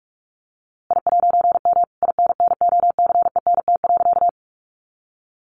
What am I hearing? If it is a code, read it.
Morse code, heard as a keyed tone, about 44 wpm, 700 Hz. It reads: I'M INNOCENT.
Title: A Sound Of Thunder 45wpm